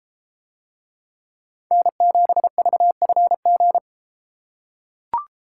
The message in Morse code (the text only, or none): N7VFG